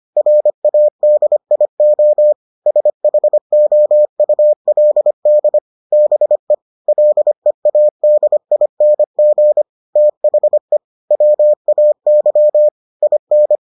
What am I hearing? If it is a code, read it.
Morse code, heard as a keyed tone, about 25 wpm, 600 Hz. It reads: RADIO SHOULD BE LEADING THE WAY IN